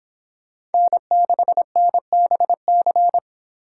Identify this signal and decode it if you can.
Morse code, heard as a keyed tone, about 31 wpm, 700 Hz. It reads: N6NBC